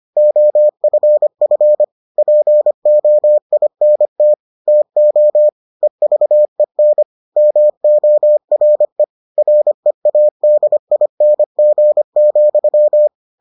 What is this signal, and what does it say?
Morse code, heard as a keyed tone, about 25 wpm, 600 Hz. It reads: OFF POINT TO EVEN MORE READING,